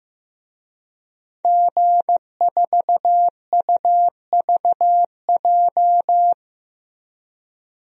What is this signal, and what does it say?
Morse code, heard as a keyed tone, about 15 wpm, 700 Hz. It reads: G4UVJ
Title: DXCC-100 Call Signs - 5 Characters - Repeated 3x - 15wpm